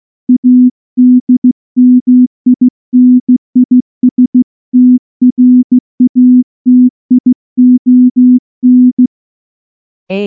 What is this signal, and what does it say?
Morse code, heard as a keyed tone, about 15 wpm, 250 Hz. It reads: ADMINISTRATION
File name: morse-training-15WPM